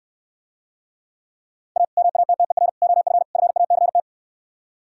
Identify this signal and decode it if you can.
Morse code, heard as a keyed tone, about 67 wpm, 700 Hz. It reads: A MATTER OF FACT